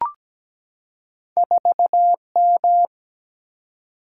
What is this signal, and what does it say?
Morse code, heard as a keyed tone, about 17 wpm, 700 Hz. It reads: 4M
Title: Single-Letter Single-Number 17wpm